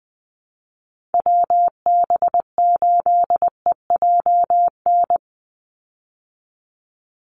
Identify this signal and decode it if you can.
Morse code, heard as a keyed tone, about 20 wpm, 700 Hz. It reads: WB8EJN
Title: Call Signs Repeated 3x - 20wpm